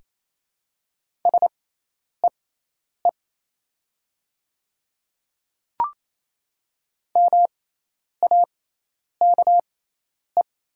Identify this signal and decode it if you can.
Morse code, heard as a keyed tone, about 28 wpm, spaced as for 10 wpm, 700 Hz. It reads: SEE MAKE
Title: 6x Character Spacing - Top 500 Words - 28wpm